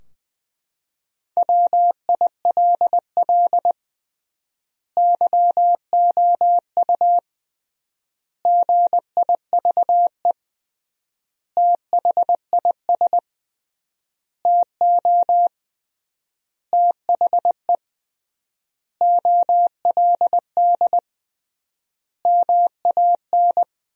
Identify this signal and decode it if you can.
Morse code, heard as a keyed tone, about 20 wpm, 700 Hz. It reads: WILL YOU GIVE THIS TO THE OLD MAN
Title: 3x Word-Spacing - Sentences from Top 200 Words - 20wpm